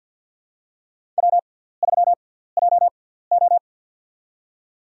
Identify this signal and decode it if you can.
Morse code, heard as a keyed tone, about 48 wpm, spaced as for 17 wpm, 700 Hz. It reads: W3JO
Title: USA Call Signs - 4 Characters - 6x Character Spacing - 50wpm